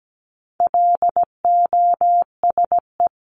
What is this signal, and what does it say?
Morse code, heard as a keyed tone, about 17 wpm, 700 Hz. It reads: LOSE
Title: Top 1001-1250 Words 17wpm